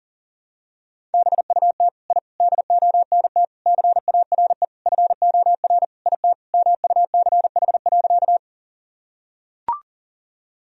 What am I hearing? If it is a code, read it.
Morse code, heard as a keyed tone, about 37 wpm, 700 Hz. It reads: BUT I DONT CARE FOR IT MUCH.
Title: Sentences from Top 1000 Words - Review Part 1 - 40wpm